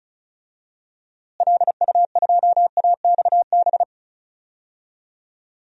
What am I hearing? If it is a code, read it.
Morse code, heard as a keyed tone, about 34 wpm, 700 Hz. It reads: LU2AXB
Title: DXCC-100 Call Signs - 6 Characters - Repeated 3x - 35wpm